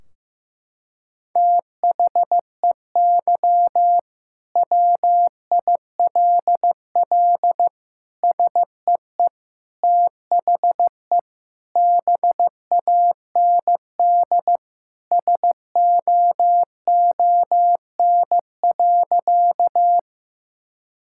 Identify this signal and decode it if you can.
Morse code, heard as a keyed tone, about 16 wpm, 700 Hz. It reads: THEY WILL SEE THE BAND SOON.
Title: Sentences from Top 1600 Words 15wpm